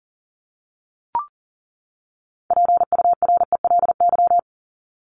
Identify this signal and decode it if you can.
Morse code, heard as a keyed tone, about 40 wpm, 700 Hz. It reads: PURELY
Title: LY Suffix - Top 100 Words with Base Words 40wpm